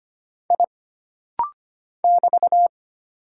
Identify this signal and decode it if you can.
Morse code, heard as a keyed tone, about 25 wpm, 700 Hz. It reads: I <BT>
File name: Single Character (Letter:Number:Punctuation:Prosign) - Rapid-Fire - 25wpm